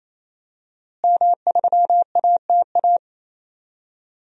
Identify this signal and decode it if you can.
Morse code, heard as a keyed tone, about 28 wpm, 700 Hz. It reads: M3ATA
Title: DXCC-100 Call Signs - 5 Characters - Repeated 3x - 28wpm